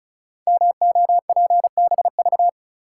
Morse code, heard as a keyed tone, about 36 wpm, 700 Hz. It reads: MOPBV
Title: Code Groups - 5 Letters - 35wpm